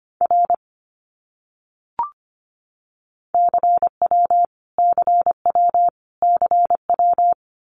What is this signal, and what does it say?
Morse code, heard as a keyed tone, about 25 wpm, 700 Hz. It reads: R CW CW CW
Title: Abbreviations Repeated 3x - 25wpm